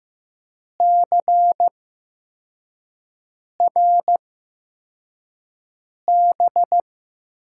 Morse code, heard as a keyed tone, about 15 wpm, spaced as for 4 wpm, 700 Hz. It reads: CRB
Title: USA Call Signs - 6 Characters - 8x Character Spacing - 15wpm